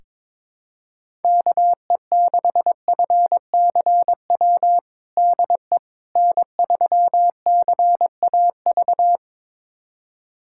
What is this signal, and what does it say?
Morse code, heard as a keyed tone, about 22 wpm, 700 Hz. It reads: KE6FCW DE N3CAV
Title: USA Callsign DE Callsign 22wpm